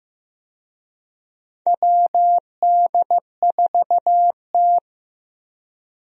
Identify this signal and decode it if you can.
Morse code, heard as a keyed tone, about 15 wpm, 700 Hz. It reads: WD4T
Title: USA Call Signs - 4 Characters - Repeated 3x - 15wpm